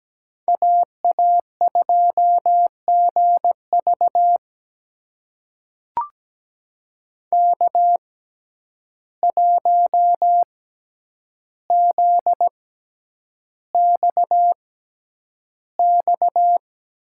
Morse code, heard as a keyed tone, about 17 wpm, spaced as for 9 wpm, 700 Hz. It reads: AA2GV K1ZXX
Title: USA Call Signs - 5 Characters - 6x Character Spacing - 17wpm